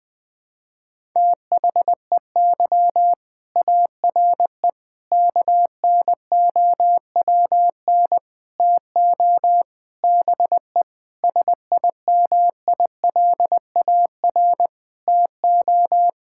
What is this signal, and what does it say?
Morse code, heard as a keyed tone, about 20 wpm, 700 Hz. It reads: THEY ARE KNOWN TO BE SIMILAR TO
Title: Sentences from Top 1000 Words - Review Part 5 - 20wpm